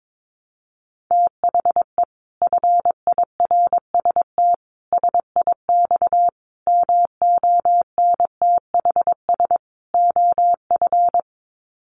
Morse code, heard as a keyed tone, about 22 wpm, 700 Hz. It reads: THE FIRST SIX MONTHS OF
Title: Sets of 5 Words Encore2 22wpm